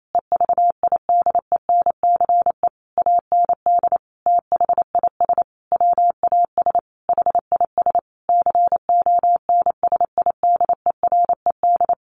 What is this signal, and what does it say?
Morse code, heard as a keyed tone, about 28 wpm, 700 Hz. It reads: EVIDENCE AND THIS WAS HIS CONSIDERED
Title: A Sound Of Thunder 28wpm